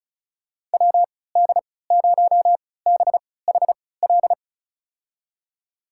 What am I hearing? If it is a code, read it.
Morse code, heard as a keyed tone, about 36 wpm, spaced as for 20 wpm, 700 Hz. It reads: WD0BHL
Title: USA Call Signs - 6 Characters - 3x Character Spacing - 35wpm